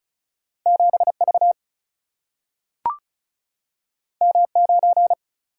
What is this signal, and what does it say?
Morse code, heard as a keyed tone, about 36 wpm, 700 Hz. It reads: 7 V M 9
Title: Single-Letter Single-Number 35wpm